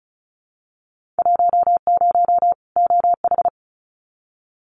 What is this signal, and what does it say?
Morse code, heard as a keyed tone, about 34 wpm, 700 Hz. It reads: 10 OH
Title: Winter Field Day Exchanges 35wpm